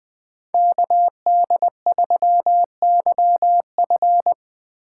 Morse code, heard as a keyed tone, about 20 wpm, 700 Hz. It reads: KD3YF
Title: Speed Racing - USA 2x2 Call Signs 17wpm